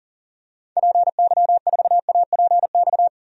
Morse code, heard as a keyed tone, about 40 wpm, 700 Hz. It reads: PY4APX
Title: DXCC-100 Call Signs - 6 Characters - Repeated 3x - 40wpm